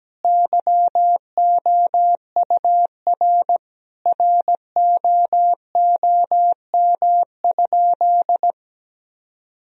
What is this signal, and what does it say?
Morse code, heard as a keyed tone, about 17 wpm, 700 Hz. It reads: YOUR ROOM?
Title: Sentences from Top 1000 Words - Review Part 4 - 17wpm